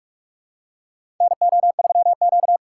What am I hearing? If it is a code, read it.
Morse code, heard as a keyed tone, about 46 wpm, 700 Hz. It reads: NO3Q
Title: USA Call Signs - 4 Characters - 3x Character Spacing - 45wpm